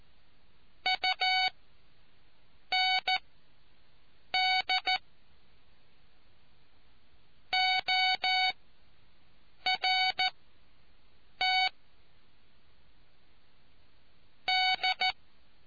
Morse code, heard as a keyed tone, about 14 wpm, spaced as for 6 wpm, 750 Hz. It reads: UND ORT D